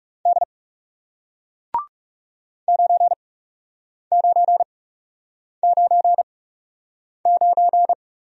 Morse code, heard as a keyed tone, about 41 wpm, spaced as for 8 wpm, 700 Hz. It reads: D 9999